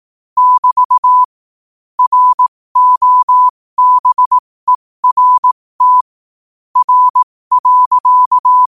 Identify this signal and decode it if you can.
Morse code, heard as a keyed tone, about 18 wpm, spaced as for 15 wpm, 1000 Hz. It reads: <BT> ROBERT R.
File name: qotdfiles / 15wpm